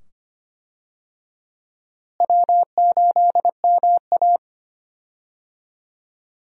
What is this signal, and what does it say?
Morse code, heard as a keyed tone, about 25 wpm, 700 Hz. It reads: W8MA
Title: USA Call Signs - 4 Characters - Repeated 2x - 25wpm